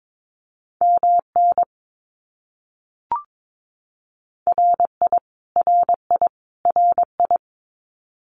Morse code, heard as a keyed tone, about 22 wpm, 700 Hz. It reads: MN RI RI RI
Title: US States Repeated 3x 22wpm